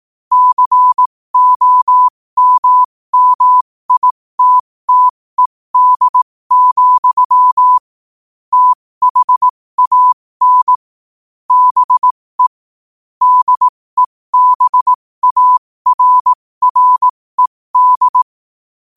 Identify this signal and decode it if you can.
Morse code, heard as a keyed tone, about 18 wpm, spaced as for 15 wpm, 1000 Hz. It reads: COMMITTED, THAN BE DEBARRED
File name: qotdfiles / 15wpm